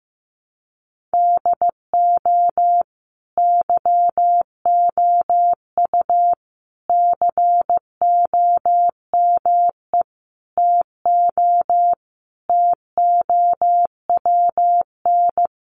Morse code, heard as a keyed tone, about 15 wpm, 700 Hz. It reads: DO YOU COME TO TOWN
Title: Sentences from Top 1400 Words 15wpm